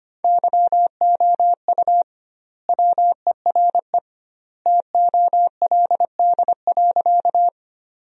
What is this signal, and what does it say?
Morse code, heard as a keyed tone, about 25 wpm, 700 Hz. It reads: YOU WERE TOLD.
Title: 2x Word-Spacing - Sentences from Top 400 Words - 25wpm